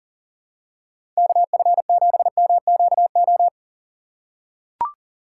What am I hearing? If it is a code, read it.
Morse code, heard as a keyed tone, about 40 wpm, 700 Hz. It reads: KF7MQO
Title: USA Call Signs - 6 Characters - 4x Character Spacing - 40wpm